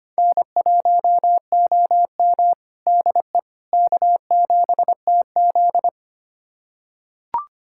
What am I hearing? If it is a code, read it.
Morse code, heard as a keyed tone, about 25 wpm, 700 Hz. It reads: N1OM DE K7TZ
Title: USA Callsign DE Callsign - 4 Characters - 25wpm